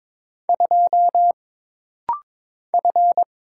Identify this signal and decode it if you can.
Morse code, heard as a keyed tone, about 22 wpm, 700 Hz. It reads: 2 F
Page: This rapid-fire timing is geared to push your ICR (Instant Character Recognition) skills.